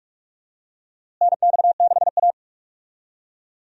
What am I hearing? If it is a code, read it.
Morse code, heard as a keyed tone, about 46 wpm, 700 Hz. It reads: NX6A
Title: USA Call Signs - 4 Characters - Repeated 3x - 45wpm